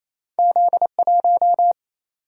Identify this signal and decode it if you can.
Morse code, heard as a keyed tone, about 28 wpm, 700 Hz. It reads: Z 1
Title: Single-Letter Single-Number 28wpm